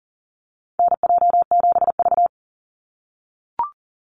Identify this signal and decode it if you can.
Morse code, heard as a keyed tone, about 40 wpm, 700 Hz. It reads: NJ7V
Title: NA SOTA Chaser Callsigns 40wpm